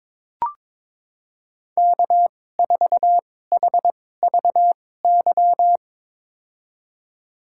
Morse code, heard as a keyed tone, about 22 wpm, spaced as for 16 wpm, 700 Hz. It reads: K4HVY
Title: USA Call Signs - 5 Characters - 2x Character Spacing - 22wpm